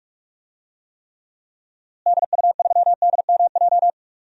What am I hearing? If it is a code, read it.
Morse code, heard as a keyed tone, about 41 wpm, 700 Hz. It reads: DU3DMJ
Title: DXCC-100 Call Signs - 6 Characters - Repeated 2x - 45wpm